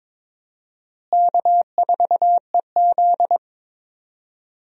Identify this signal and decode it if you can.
Morse code, heard as a keyed tone, about 22 wpm, 700 Hz. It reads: K4EZ
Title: USA Call Signs - 4 Characters - Repeated 3x - 22wpm